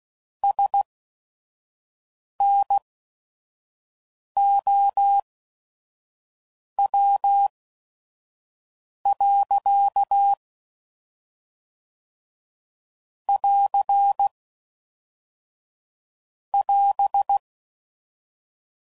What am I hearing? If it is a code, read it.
Morse code, heard as a keyed tone, about 16 wpm, spaced as for 5 wpm, 800 Hz. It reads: SNOW. <AR> <AS>
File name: CW Code Groups @5WPM / Weather Sets / Weather 3 17